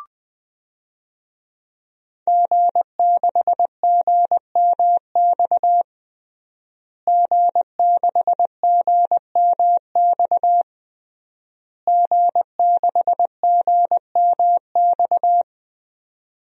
Morse code, heard as a keyed tone, about 20 wpm, 700 Hz. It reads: G6GMX G6GMX G6GMX
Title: DXCC-100 Call Signs - 5 Characters - Repeated 3x - 20wpm